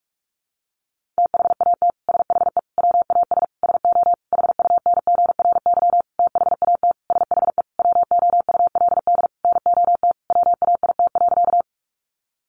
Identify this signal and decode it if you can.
Morse code, heard as a keyed tone, about 46 wpm, 700 Hz. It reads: THAT SHE WAS SO HUNGRY THAT SHE WOULD NOT WAIT.
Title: Sentences from Top 1400 Words 45wpm